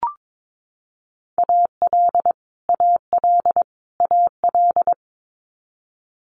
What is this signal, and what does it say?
Morse code, heard as a keyed tone, about 22 wpm, 700 Hz. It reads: AL AL AL
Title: Multi-letter Phonemes Repeated 3x 22wpm